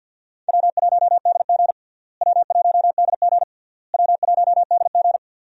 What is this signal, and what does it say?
Morse code, heard as a keyed tone, about 47 wpm, 700 Hz. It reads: W1DG W1DG W1DG
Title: USA Call Signs - 4 Characters - Repeated 3x - 50wpm